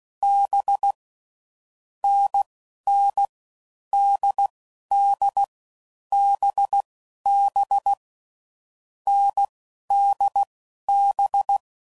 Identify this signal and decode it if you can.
Morse code, heard as a keyed tone, about 16 wpm, 800 Hz. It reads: B N N D D B B N D B
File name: Combinations / Difficult Combos / NDB NDB6